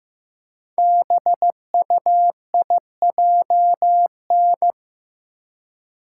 Morse code, heard as a keyed tone, about 15 wpm, 700 Hz. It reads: BUIJN
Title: Code Groups - 5 Letters - 15wpm